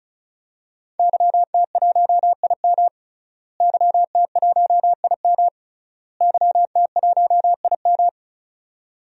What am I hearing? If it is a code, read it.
Morse code, heard as a keyed tone, about 34 wpm, 700 Hz. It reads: YT1IM YT1IM YT1IM
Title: DXCC-100 Call Signs - 5 Characters - Repeated 3x - 35wpm